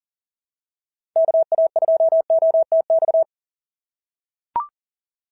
Morse code, heard as a keyed tone, about 40 wpm, 690 Hz. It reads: KA2OTX
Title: SOTA QSO Practice 40wpm